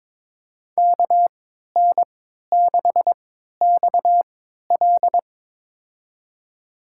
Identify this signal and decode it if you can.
Morse code, heard as a keyed tone, about 22 wpm, spaced as for 12 wpm, 700 Hz. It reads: KN6XL
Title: USA Call Signs - 5 Characters - 3x Character Spacing - 22wpm